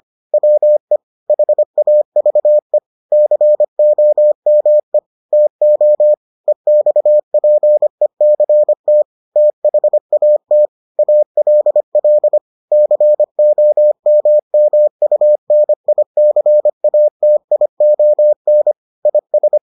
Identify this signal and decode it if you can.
Morse code, heard as a keyed tone, about 25 wpm, 600 Hz. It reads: WE HAVE COME TO EXPECT THAT ALL COMMUNICATION IS